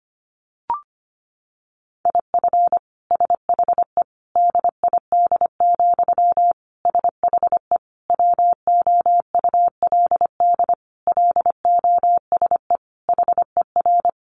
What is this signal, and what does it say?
Morse code, heard as a keyed tone, about 25 wpm, 700 Hz. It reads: IF SHE DID, SHE WOULD LOSE HER
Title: Sentences from Top 1400 Words 25wpm